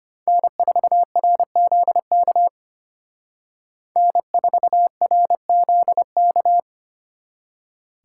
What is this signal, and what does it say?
Morse code, heard as a keyed tone, about 27 wpm, 700 Hz. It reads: N4RZK N4RZK
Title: Speed Racing - USA 1x3 Call Signs 25wpm